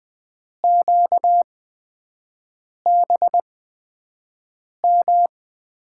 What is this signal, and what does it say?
Morse code, heard as a keyed tone, about 20 wpm, spaced as for 5 wpm, 700 Hz. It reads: QBM
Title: USA Call Signs - 8x Character Spacing - 20wpm